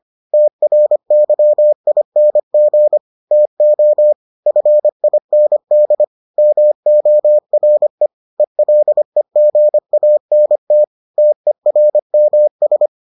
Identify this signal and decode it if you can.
Morse code, heard as a keyed tone, about 25 wpm, 600 Hz. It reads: TRYING TO FIND MORE ELEGANT TERMS